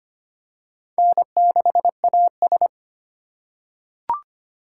Morse code, heard as a keyed tone, about 25 wpm, 700 Hz. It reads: N6AS
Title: USA Call Signs - 4 Characters - 3x Character Spacing - 25wpm